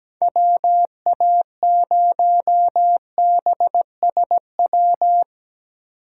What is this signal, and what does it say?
Morse code, heard as a keyed tone, about 17 wpm, 700 Hz. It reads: WA0BSW
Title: USA Call Signs - 6 Characters - 3x Character Spacing - 17wpm